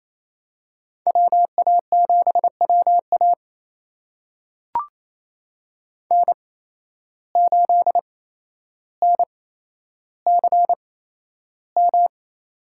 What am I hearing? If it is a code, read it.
Morse code, heard as a keyed tone, about 28 wpm, spaced as for 12 wpm, 700 Hz. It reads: WA7WA N8NCM
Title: USA Call Signs - 5 Characters - 8x Character Spacing - 28wpm